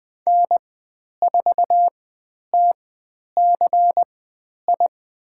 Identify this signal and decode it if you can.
Morse code, heard as a keyed tone, about 20 wpm, spaced as for 10 wpm, 700 Hz. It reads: N4TCI
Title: 109 Lesson 18g - Letter m - Call Signs 20wpm